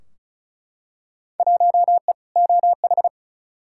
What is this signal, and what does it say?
Morse code, heard as a keyed tone, about 36 wpm, 700 Hz. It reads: 1E OH
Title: Field Day Exchanges - 35wpm